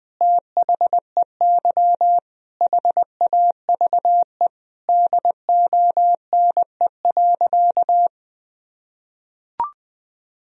Morse code, heard as a keyed tone, about 20 wpm, 700 Hz. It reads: THEY HAVE DONE.
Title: Sentences from Top 1000 Words - Review Part 3 - 20wpm